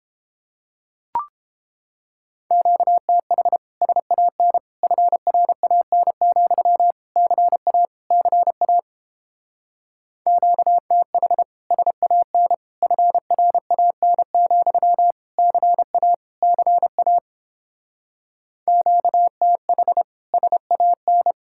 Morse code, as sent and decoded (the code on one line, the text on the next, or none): --.- - .... / ... .- -. / ..-. .-. .- -. --..-- / -.-. .- / -.-. .- / --.- - .... / ... .- -. / ..-. .-. .- -. --..-- / -.-. .- / -.-. .- / --.- - .... / ... .- -.
QTH SAN FRAN, CA CA QTH SAN FRAN, CA CA QTH SAN